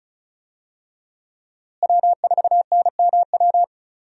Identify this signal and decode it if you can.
Morse code, heard as a keyed tone, about 36 wpm, 700 Hz. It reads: W4NMW
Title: USA Call Signs - 5 Characters - Repeated 3x - 35wpm